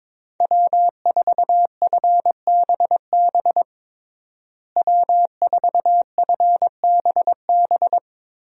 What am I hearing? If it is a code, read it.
Morse code, heard as a keyed tone, about 22 wpm, 700 Hz. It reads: W4FBB W4FBB
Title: USA Call Signs - 5 Characters - Repeated 2x - 22wpm